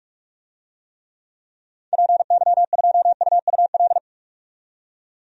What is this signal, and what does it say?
Morse code, heard as a keyed tone, about 41 wpm, 700 Hz. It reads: PY2UUL
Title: DXCC-100 Call Signs - 6 Characters - 45wpm